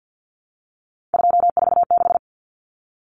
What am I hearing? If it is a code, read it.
Morse code, heard as a keyed tone, about 50 wpm, 700 Hz. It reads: 246
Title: Numbers - 3-Digit 50wpm